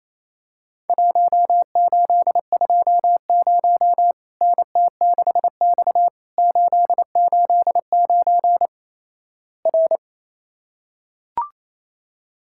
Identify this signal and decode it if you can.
Morse code, heard as a keyed tone, about 28 wpm, 700 Hz. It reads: 1820 NT6X 889 R
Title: Worked All Europe DX Contest - QTC Exchange - 28wpm